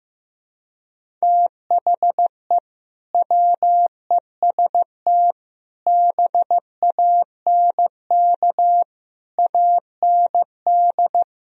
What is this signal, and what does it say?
Morse code, heard as a keyed tone, about 16 wpm, 700 Hz. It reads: THE WEST BANK AND
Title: Speed-Racing - Sets of 5 Words 15wpm